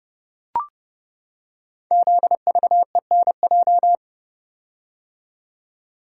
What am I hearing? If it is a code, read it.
Morse code, heard as a keyed tone, about 30 wpm, 700 Hz. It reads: ZVENJ
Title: Code Groups - 5 Letters - 30wpm